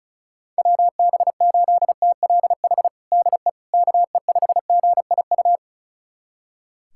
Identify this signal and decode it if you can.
Morse code, heard as a keyed tone, about 36 wpm, 700 Hz. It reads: WB8TLH DE KE5GIU
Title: USA Callsign DE Callsign - 6 Characters - 35wpm